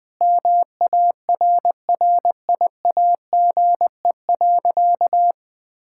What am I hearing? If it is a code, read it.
Morse code, heard as a keyed tone, about 21 wpm, 700 Hz. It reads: MARRIAGE.
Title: Sentences from Top 1000 Words - Review Part 5 - 20wpm